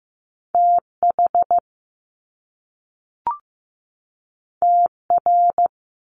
Morse code, heard as a keyed tone, about 15 wpm, 700 Hz. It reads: TH TR
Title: Top Two Letter Combinations in Words 15wpm